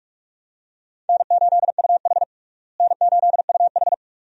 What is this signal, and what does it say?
Morse code, heard as a keyed tone, about 44 wpm, 700 Hz. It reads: N8UH N8UH
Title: USA Call Signs - 4 Characters - Repeated 2x - 45wpm